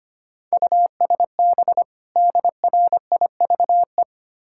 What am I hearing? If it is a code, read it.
Morse code, heard as a keyed tone, about 27 wpm, 700 Hz. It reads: USB DRIVE
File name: Easy Words - Speed-Racing with Hints - 25wpm